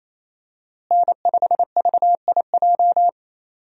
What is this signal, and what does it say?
Morse code, heard as a keyed tone, about 28 wpm, 700 Hz. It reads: N5VIJ
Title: USA Call Signs - 5 Characters - Repeated 2x - 28wpm